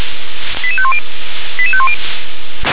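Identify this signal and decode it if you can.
Five-tone selective call (ZVEI), heard as quick runs of five tones, tones of about 80 ms. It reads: 88510 88510
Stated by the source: Five-tone ZVEI selective call received by SDR.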